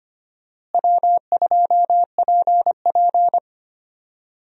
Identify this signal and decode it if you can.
Morse code, heard as a keyed tone, about 25 wpm, 700 Hz. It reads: W2PP
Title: USA Call Signs - 4 Characters - Repeated 3x - 25wpm